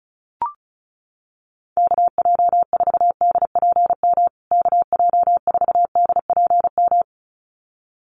Morse code, heard as a keyed tone, about 34 wpm, 700 Hz. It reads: KJ4DPM KJ4DPM
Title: USA Call Signs Repeated 2x 35wpm